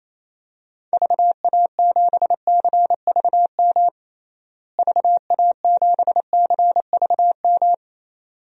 Morse code, heard as a keyed tone, about 28 wpm, 700 Hz. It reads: VA7CVM VA7CVM
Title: DXCC-100 Call Signs - 6 Characters - Repeated 2x - 28wpm